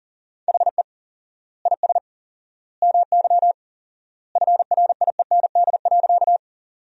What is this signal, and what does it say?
Morse code, heard as a keyed tone, about 40 wpm, 700 Hz. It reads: HE IS MY FRIEND.
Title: 4x Word-Spacing - Sentences from Top 300 Words - 40wpm